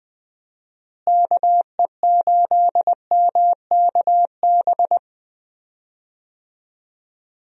Morse code, heard as a keyed tone, about 20 wpm, 700 Hz. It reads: KE8MKB
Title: USA Call Signs - 6 Characters - Repeated 2x - 20wpm